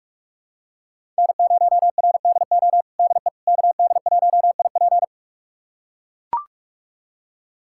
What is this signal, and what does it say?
Morse code, heard as a keyed tone, about 41 wpm, 700 Hz. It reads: N0RDO DE KD1IP
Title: USA Callsign DE Callsign - 5 Characters - 45wpm